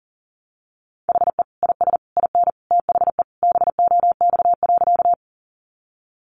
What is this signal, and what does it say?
Morse code, heard as a keyed tone, about 40 wpm, 700 Hz. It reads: HE IS IN THE BOX.
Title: Sentences from Top 500 Words 40wpm